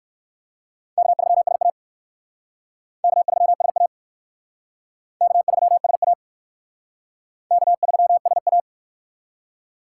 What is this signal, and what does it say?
Morse code, heard as a keyed tone, about 60 wpm, 700 Hz. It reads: K3SA K3SA K3SA K3SA